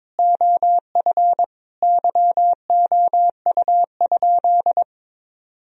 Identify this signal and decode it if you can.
Morse code, heard as a keyed tone, about 22 wpm, 700 Hz. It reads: OF YOU?
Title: Sentences from Top 1000 Words - Review Part 1 - 22wpm